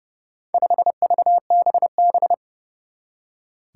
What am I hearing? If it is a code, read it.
Morse code, heard as a keyed tone, about 32 wpm, 700 Hz. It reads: 5VBB
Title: Four Challenging Letter-Number Combinations 30wpm